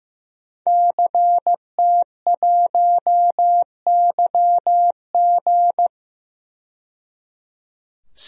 Morse code, heard as a keyed tone, about 15 wpm, 700 Hz. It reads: CT1YG
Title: DXCC-100 Call Signs - 5 Characters - 15wpm